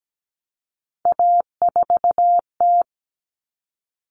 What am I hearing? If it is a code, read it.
Morse code, heard as a keyed tone, about 17 wpm, 700 Hz. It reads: A4T
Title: All Cut Numbers - 3-Digit 17wpm